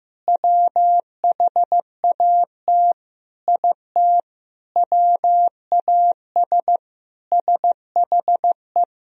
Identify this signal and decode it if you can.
Morse code, heard as a keyed tone, about 16 wpm, 700 Hz. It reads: WHAT IT WAS SHE
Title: Sentences from Top 400 Words and Abbreviations 15wpm